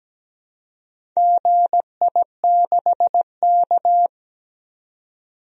Morse code, heard as a keyed tone, about 17 wpm, 700 Hz. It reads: GI6K
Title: DXCC-100 Call Signs - 4 Characters - 17wpm